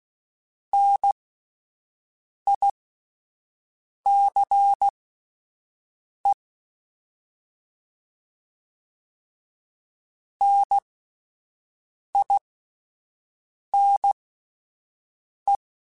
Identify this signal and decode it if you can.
Morse code, heard as a keyed tone, about 16 wpm, spaced as for 5 wpm, 800 Hz. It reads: NICE NINE
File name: Alphabetical / Alphabetical mp3s / Ns @5